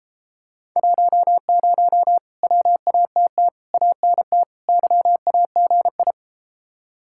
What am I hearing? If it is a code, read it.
Morse code, heard as a keyed tone, about 33 wpm, 700 Hz. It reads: 10 WATT ANT YAGI